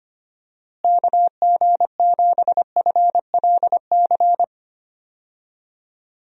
Morse code, heard as a keyed tone, about 25 wpm, 700 Hz. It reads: KG7FLC
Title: USA Call Signs - 6 Characters - Repeated 2x - 25wpm